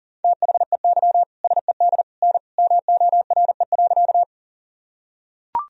Morse code, heard as a keyed tone, about 40 wpm, 700 Hz. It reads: THEY SED N MORE.
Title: Sentences from Top 100 Words and Abbreviations 40wpm